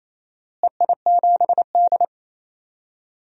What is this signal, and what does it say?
Morse code, heard as a keyed tone, about 28 wpm, 700 Hz. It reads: EI7D
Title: DXCC-100 Call Signs - 4 Characters - Repeated 2x - 28wpm